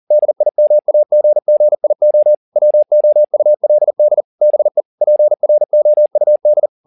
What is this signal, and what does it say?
Morse code, heard as a keyed tone, about 40 wpm, 600 Hz. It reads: DIMAGGIO WOULD BE PROUD
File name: the old man and the sea 2 40wpm